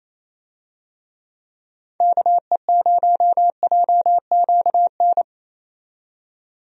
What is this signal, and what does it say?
Morse code, heard as a keyed tone, about 28 wpm, 700 Hz. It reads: KE0JQN
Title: USA Call Signs - 6 Characters - 28wpm